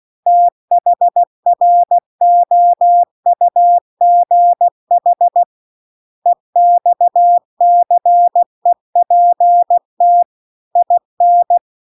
Morse code, heard as a keyed tone, about 16 wpm, 700 Hz. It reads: THROUGH EXCEPT IN
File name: CW-The-Big-Twoheaded-River-2-16WPM